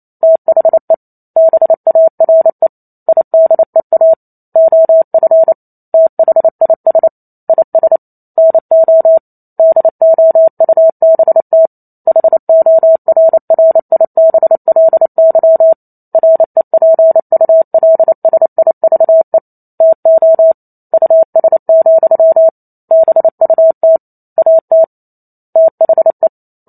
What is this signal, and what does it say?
Morse code, heard as a keyed tone, about 28 wpm, 650 Hz. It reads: THE BARE IDEA OF THIS IS NO DOUBT HORRIBLY REPULSIVE TO US, BUT AT THE